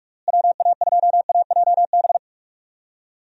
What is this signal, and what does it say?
Morse code, heard as a keyed tone, about 42 wpm, 700 Hz. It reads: WA2AJB
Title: USA Call Signs - 6 Characters - Repeated 3x - 45wpm